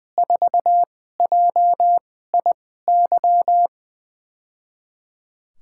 Morse code, heard as a keyed tone, about 21 wpm, spaced as for 15 wpm, 700 Hz. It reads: 4JIY
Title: USA Call Signs - 6 Characters - 2x Character Spacing - 20wpm